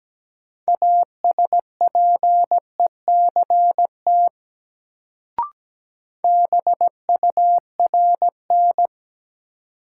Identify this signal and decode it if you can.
Morse code, heard as a keyed tone, about 17 wpm, 700 Hz. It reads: ASPECT BURN
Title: Sentences from Top 1300 Words - New Word Drill - 17wpm